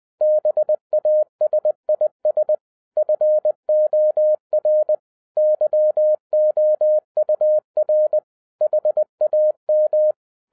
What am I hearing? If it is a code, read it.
Morse code, heard as a keyed tone, about 20 wpm, 610 Hz. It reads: BASIS FOR YOUR HAM
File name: 2011-10-mp3 / n13-TIPS FROM MY 34 YEARS OF EXPERIENCE 20wpm